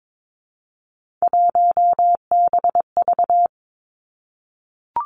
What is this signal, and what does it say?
Morse code, heard as a keyed tone, about 22 wpm, 700 Hz. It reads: 1BV
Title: Three Challenging Letter-Number Combinations 22wpm